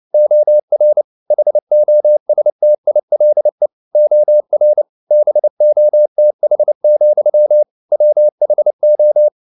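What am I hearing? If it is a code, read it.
Morse code, heard as a keyed tone, about 29 wpm, 600 Hz. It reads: OR HOSTILE OR BOTH, WHO
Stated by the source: Morse code at 29 wpm, sending Ken Follett's novel Never in English, from CW med Gnister.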